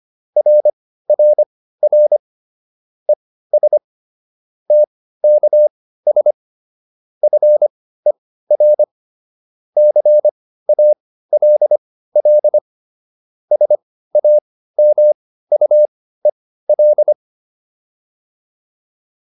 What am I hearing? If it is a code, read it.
Morse code, heard as a keyed tone, about 25 wpm, spaced as for 15 wpm, 600 Hz. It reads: RRR ES TKS FER CALL SAMUEL